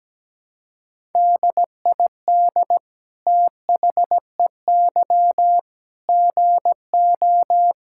Morse code, heard as a keyed tone, about 17 wpm, 700 Hz. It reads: DID THEY GO
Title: Sentences from Top 1000 Words - Review Part 5 - 17wpm